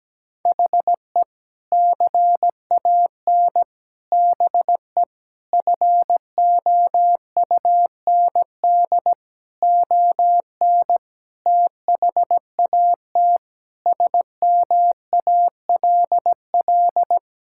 Morse code, heard as a keyed tone, about 17 wpm, 700 Hz. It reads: HE CAN BE FOUND ON THAT SMALL
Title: Sentences from Top 1600 Words 17wpm